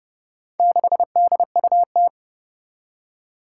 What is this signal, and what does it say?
Morse code, heard as a keyed tone, about 30 wpm, 700 Hz. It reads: 6DUT
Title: Four Challenging Letter-Number Combinations 30wpm